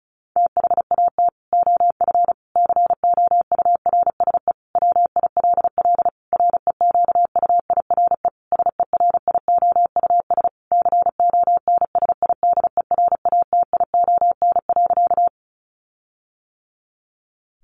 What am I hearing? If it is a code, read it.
Morse code, heard as a keyed tone, about 34 wpm, 700 Hz. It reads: THAT OF COURSE WILL REQUIRE SERIOUS CONSIDERATION.
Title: Sentences from Top 1500 Words 35wpm